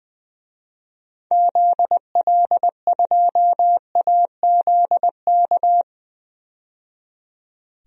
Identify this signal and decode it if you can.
Morse code, heard as a keyed tone, about 20 wpm, 700 Hz. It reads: ZL2AZK
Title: DXCC-100 Call Signs - 6 Characters - 20wpm